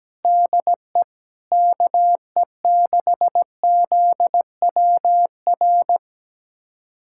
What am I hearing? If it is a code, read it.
Morse code, heard as a keyed tone, about 17 wpm, 700 Hz. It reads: DE KE6ZWR
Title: USA Callsign DE Callsign - 6 Characters - 17wpm